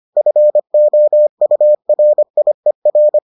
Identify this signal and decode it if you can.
Morse code, heard as a keyed tone, about 25 wpm, 600 Hz. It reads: FOURIER